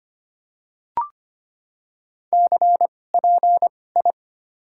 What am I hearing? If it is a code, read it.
Morse code, heard as a keyed tone, about 25 wpm, spaced as for 18 wpm, 700 Hz. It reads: CPI
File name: Top 90 QSO Elements - 2x Character Spacing - 25wpm